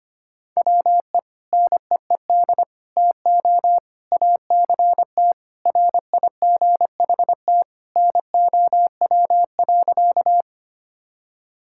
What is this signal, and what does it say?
Morse code, heard as a keyed tone, about 25 wpm, 700 Hz. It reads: WE NEED TO ACT RIGHT NOW.
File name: practice / 25wpm / Sentences with 2-Word Phrases - 25wpm